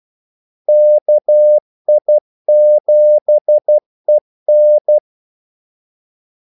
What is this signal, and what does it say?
Morse code, heard as a keyed tone, about 12 wpm, 600 Hz. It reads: KI7EN
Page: 2 ekstra ordmellemrum mellem kaldesignalerne